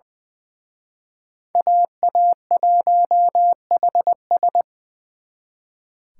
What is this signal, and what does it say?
Morse code, heard as a keyed tone, about 24 wpm, 700 Hz. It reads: AA1HS
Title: Speed Racing - USA 2x2 Call Signs 20wpm